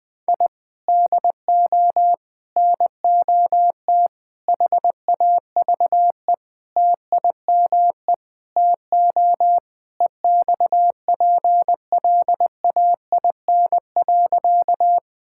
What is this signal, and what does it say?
Morse code, heard as a keyed tone, about 21 wpm, 700 Hz. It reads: I DO NOT HAVE TIME TO EXPLAIN.
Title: Sentences with 5-Word Phrases - 20wpm